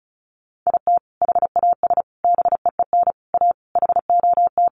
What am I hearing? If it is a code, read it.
Morse code, heard as a keyed tone, about 34 wpm, 700 Hz. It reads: IT HAS BEEN A HOT
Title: Sentences from Top 1100 Words 35wpm